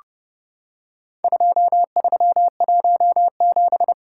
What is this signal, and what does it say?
Morse code, heard as a keyed tone, about 29 wpm, 700 Hz. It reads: 2317
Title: Worked All Europe DX Contest - QTC Exchange - 30wpm